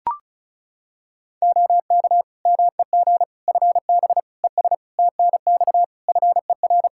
Morse code, heard as a keyed tone, about 36 wpm, 700 Hz. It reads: OK MEG FB ES TNX FER
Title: Practice QSO - Exchange 2 - Info - 35wpm